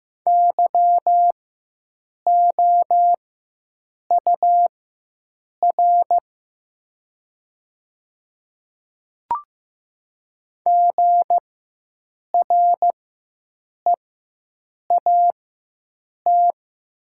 Morse code, heard as a keyed tone, about 15 wpm, spaced as for 7 wpm, 700 Hz. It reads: YOUR GREAT
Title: 4x Character Spacing - Top 300 Words - 15wpm